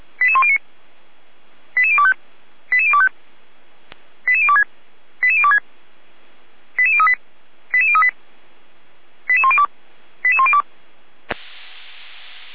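Five-tone selective call (ZVEI), heard as quick runs of five tones, tones of about 70 ms. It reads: 88109 88025 88025 88026 88026 88038 88038 88102 88102
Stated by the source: ZVEI five-tone call received by SDR, feuerwehr-orlamuende.org fire brigade server.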